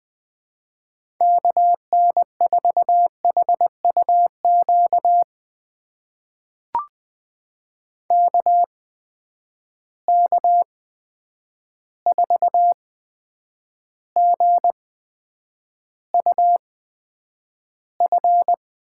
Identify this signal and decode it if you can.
Morse code, heard as a keyed tone, about 20 wpm, spaced as for 9 wpm, 700 Hz. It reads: KN4HUQ KK4GUF
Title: USA Call Signs - 6 Characters - 8x Character Spacing - 20wpm